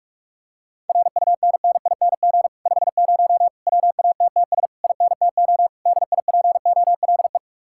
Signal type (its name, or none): Morse code